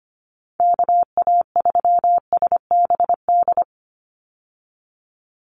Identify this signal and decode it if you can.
Morse code, heard as a keyed tone, about 25 wpm, 700 Hz. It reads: KA3SBD
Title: USA Call Signs 25wpm